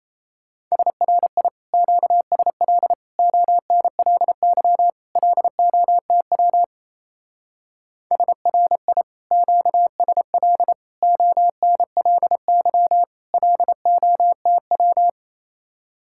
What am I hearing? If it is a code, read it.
Morse code, heard as a keyed tone, about 34 wpm, 700 Hz. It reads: SRI QSL ONLY LOTW SRI QSL ONLY LOTW
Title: Speed-Racing Common QSO Phrases Encore - 28wpm